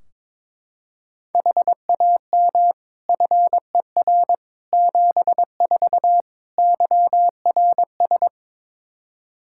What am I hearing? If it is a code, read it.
Morse code, heard as a keyed tone, about 22 wpm, 700 Hz. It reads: HAM FER 74 YRS
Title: Practice QSO - Exchange 3 - More Info - 22wpm